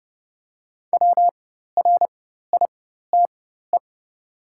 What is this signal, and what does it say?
Morse code, heard as a keyed tone, about 30 wpm, spaced as for 14 wpm, 700 Hz. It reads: WRITE
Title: 4x Character Spacing - Top 200 Words - 30wpm